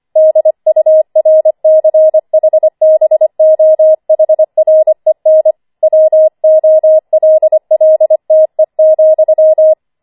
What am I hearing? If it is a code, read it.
Morse code, heard as a keyed tone, about 25 wpm, 630 Hz. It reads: DURCHBOHREN WOLLTE,
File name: 25wpm